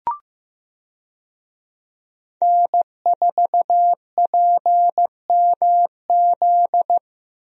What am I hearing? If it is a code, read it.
Morse code, heard as a keyed tone, about 16 wpm, 700 Hz. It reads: N4PMZ
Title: USA Call Signs - 5 Characters - Repeated 3x - 15wpm